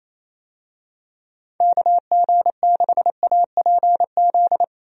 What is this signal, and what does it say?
Morse code, heard as a keyed tone, about 28 wpm, 700 Hz. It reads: KG6APZ
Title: USA Call Signs - 6 Characters - Repeated 3x - 28wpm